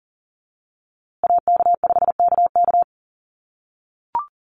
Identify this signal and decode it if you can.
Morse code, heard as a keyed tone, about 40 wpm, 700 Hz. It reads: AK5KK
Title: USA Call Signs Repeated 2x 40wpm